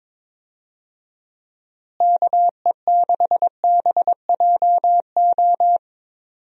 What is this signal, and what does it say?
Morse code, heard as a keyed tone, about 22 wpm, 700 Hz. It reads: KE6BJO
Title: USA Call Signs - 6 Characters - Repeated 3x - 22wpm